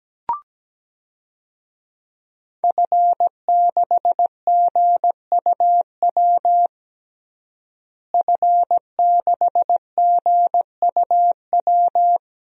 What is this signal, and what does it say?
Morse code, heard as a keyed tone, about 17 wpm, 700 Hz. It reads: F6GUW F6GUW
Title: DXCC-100 Call Signs - 5 Characters - Repeated 2x - 17wpm